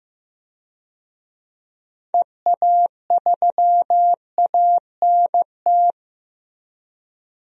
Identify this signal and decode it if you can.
Morse code, heard as a keyed tone, about 15 wpm, 700 Hz. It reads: EA3ANT
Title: DXCC-100 Call Signs - 6 Characters - 15wpm